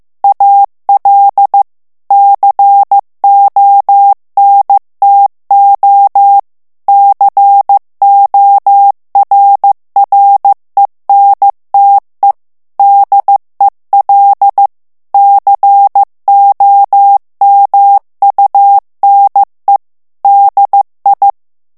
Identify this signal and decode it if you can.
Morse code, heard as a keyed tone, about 15 wpm, 800 Hz. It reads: AL CONTO CORRENTE DEL COMUNE DI
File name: ITALIEN texte clair 1 vitesse 14 mots minute